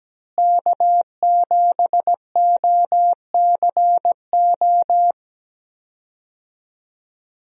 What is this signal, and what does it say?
Morse code, heard as a keyed tone, about 17 wpm, 700 Hz. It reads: K7OCO
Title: USA Call Signs - 5 Characters - Repeated 3x - 17wpm